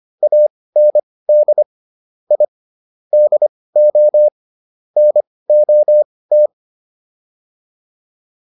Morse code, heard as a keyed tone, about 25 wpm, spaced as for 18 wpm, 600 Hz. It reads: AND I DO NOT